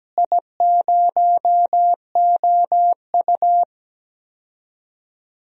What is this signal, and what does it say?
Morse code, heard as a keyed tone, about 17 wpm, 700 Hz. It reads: I0OU
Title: DXCC-100 Call Signs - 4 Characters - Repeated 2x - 17wpm